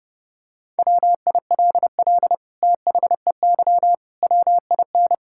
Morse code, heard as a keyed tone, about 30 wpm, 700 Hz. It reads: WILL THEY WIN
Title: Sentences from Top 1600 Words 30wpm